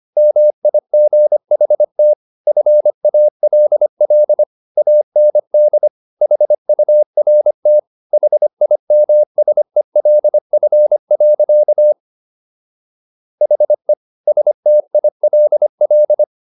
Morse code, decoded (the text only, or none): MIGHT FALL AND HURT HIMSELF. HE STILL